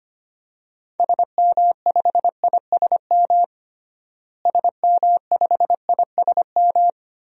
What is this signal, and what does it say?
Morse code, heard as a keyed tone, about 25 wpm, 690 Hz. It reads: SM5ISM SM5ISM
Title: DXCC-100 Call Signs - 6 Characters - Repeated 3x - 25wpm